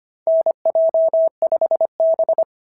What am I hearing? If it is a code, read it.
Morse code, heard as a keyed tone, about 25 wpm, 690 Hz. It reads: NJ5B
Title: SOTA QSO Practice 25wpm